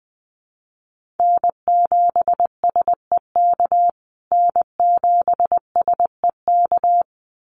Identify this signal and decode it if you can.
Morse code, heard as a keyed tone, about 20 wpm, 700 Hz. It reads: N7SEK N7SEK
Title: USA Call Signs Repeated 2x 20wpm